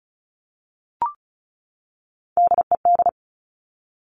Morse code, heard as a keyed tone, about 36 wpm, 700 Hz. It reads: DED
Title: Top Three Letter Combinations in Words 35wpm